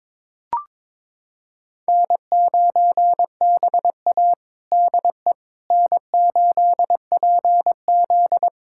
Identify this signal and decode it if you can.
Morse code, heard as a keyed tone, about 22 wpm, 700 Hz. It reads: N9BA DE N8PZ
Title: USA Callsign DE Callsign - 4 Characters - 22wpm